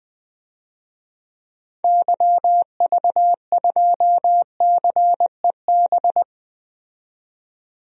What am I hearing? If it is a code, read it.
Morse code, heard as a keyed tone, about 21 wpm, 700 Hz. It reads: YV2CEB
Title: DXCC-100 Call Signs - 6 Characters - 20wpm